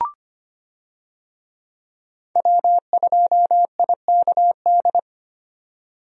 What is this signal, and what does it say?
Morse code, heard as a keyed tone, about 25 wpm, 700 Hz. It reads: W2IKD
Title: USA Call Signs - 5 Characters - 25wpm